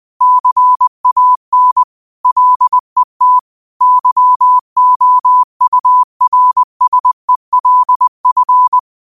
Morse code, heard as a keyed tone, about 20 wpm, 1000 Hz. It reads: CAN LET YOURSELF
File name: qotdfiles / 20wpm